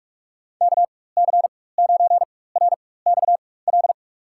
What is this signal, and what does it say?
Morse code, heard as a keyed tone, about 45 wpm, spaced as for 21 wpm, 700 Hz. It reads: KC9RXL
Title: USA Call Signs - 6 Characters - 4x Character Spacing - 45wpm